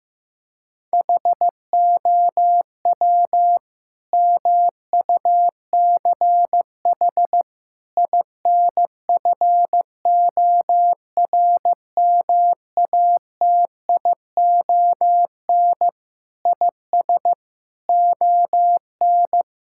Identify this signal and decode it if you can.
Morse code, heard as a keyed tone, about 15 wpm, 700 Hz. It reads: HOW MUCH INFORMATION IS ON
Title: Sentences from Top 1000 Words - Review Part 6 - 15wpm